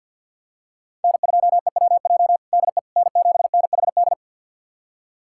Morse code, heard as a keyed tone, about 50 wpm, 700 Hz. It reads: N2EWJ DE N7NHD
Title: USA Callsign DE Callsign - 5 Characters - 50wpm